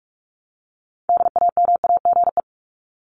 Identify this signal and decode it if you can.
Morse code, heard as a keyed tone, about 44 wpm, 700 Hz. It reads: DAMAGE
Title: Top 1001-1250 Words 45wpm